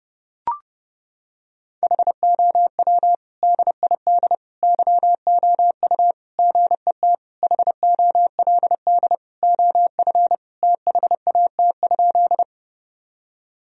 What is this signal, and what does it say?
Morse code, heard as a keyed tone, about 30 wpm, 700 Hz. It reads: HOW DID YOU GET HOLD OF THAT?
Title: Sentences from Top 1000 Words - Review Part 2 - 30wpm